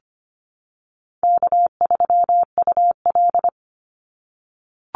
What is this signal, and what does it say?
Morse code, heard as a keyed tone, about 25 wpm, 700 Hz. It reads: K3UL
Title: Easy Call Signs - 25wpm